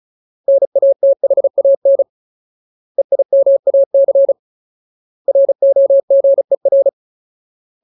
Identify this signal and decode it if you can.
Morse code, heard as a keyed tone, about 34 wpm, 550 Hz. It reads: NATHAN EIMAC ROGER
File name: Ultimate Word List / 35WPM Ultimate